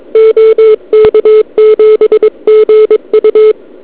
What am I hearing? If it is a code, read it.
Morse code, heard as a keyed tone, about 23 wpm, 470 Hz. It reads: OK7GU
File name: ARCHIV / KX3 / Scan 160m